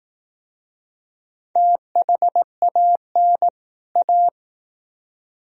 Morse code, heard as a keyed tone, about 18 wpm, 700 Hz. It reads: THAN A
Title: Speed-Racing - Sets of 2 Words 15wpm